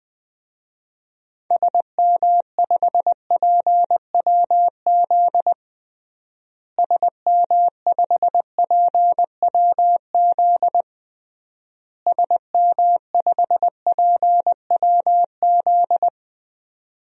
Morse code, heard as a keyed tone, about 20 wpm, 700 Hz. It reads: SM5PWZ SM5PWZ SM5PWZ
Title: DXCC-100 Call Signs - 6 Characters - Repeated 3x - 20wpm